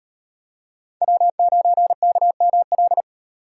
Morse code, heard as a keyed tone, about 34 wpm, 700 Hz. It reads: W9KML